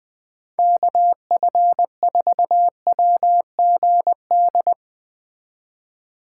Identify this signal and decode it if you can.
Morse code, heard as a keyed tone, about 19 wpm, 700 Hz. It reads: KF4WGD
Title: Speed Racing - USA 2x3 Call Signs 17wpm